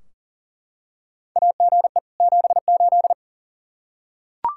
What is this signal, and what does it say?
Morse code, heard as a keyed tone, about 40 wpm, 700 Hz. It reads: AGE 78
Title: Practice QSO - Exchange 3 - More Info - 40wpm